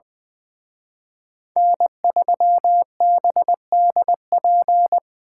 Morse code, heard as a keyed tone, about 24 wpm, 700 Hz. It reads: N3BDP
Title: Speed Racing - USA 1x3 Call Signs 20wpm